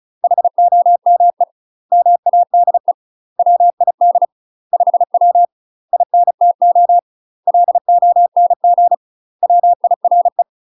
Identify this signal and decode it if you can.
Morse code, heard as a keyed tone, about 36 wpm, 700 Hz. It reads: HOME MADE WID 5W INTO LONG WIRE
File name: UK MASTER QSOs TRAINING / 35WPM / UK MASTER QSOs 0002